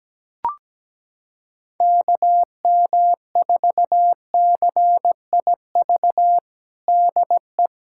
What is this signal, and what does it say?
Morse code, heard as a keyed tone, about 17 wpm, 700 Hz. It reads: KM4CIV DE
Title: USA Callsign DE Callsign 17wpm